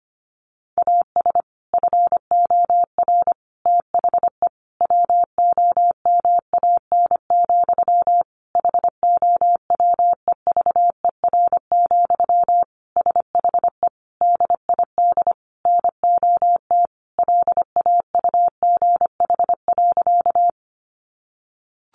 Morse code, heard as a keyed tone, about 25 wpm, 700 Hz. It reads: AS FOR THE WOMAN, HOWEVER, SHE DID NOT LAUGH.
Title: Sentences from Top 1100 Words 25wpm